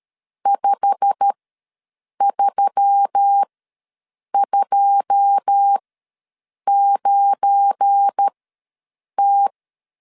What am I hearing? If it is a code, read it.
Morse code, heard as a keyed tone, about 13 wpm, spaced as for 7 wpm, 800 Hz. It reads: 5329T
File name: LEZIONI / 9 2 ws 12wpm 3406